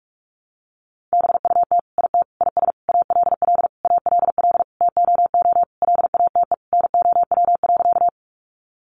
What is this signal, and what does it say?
Morse code, heard as a keyed tone, about 44 wpm, 700 Hz. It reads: BUT IT IS ALL ALL TOO LATE NOW.
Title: Sentences from Top 300 Words 45wpm